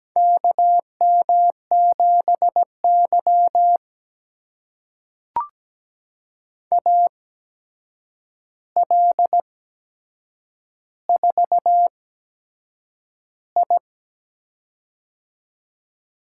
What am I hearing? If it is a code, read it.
Morse code, heard as a keyed tone, about 17 wpm, spaced as for 7 wpm, 700 Hz. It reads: KM7Y AL4I
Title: USA Call Signs - 4 Characters - 8x Character Spacing - 17wpm